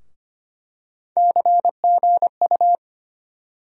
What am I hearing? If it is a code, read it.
Morse code, heard as a keyed tone, about 25 wpm, 700 Hz. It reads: CGU
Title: USA Call Sign Postfixes - 3 Letters 25wpm